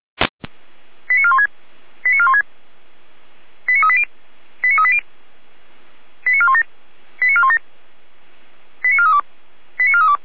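Five-tone selective call (ZVEI), heard as quick runs of five tones, tones of about 70 ms. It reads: 89416 89416 89380 89380 89417 89417 89432 89432
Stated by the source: ZVEI five-tone selective call received by SDR, from the feuerwehr-orlamuende.org fire brigade archive.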